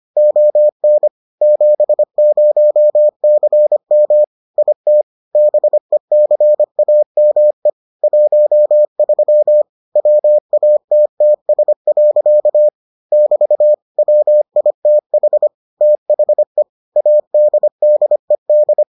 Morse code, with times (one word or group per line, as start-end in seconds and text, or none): 0.2-1.1 ON
1.4-4.2 70CM
4.6-5.0 IT
5.4-7.7 BECAME
8.0-9.6 13
10.0-12.7 WATTS.
13.1-13.7 <BT>
14.0-15.5 WITH
15.8-16.6 THE
17.0-18.8 ADDED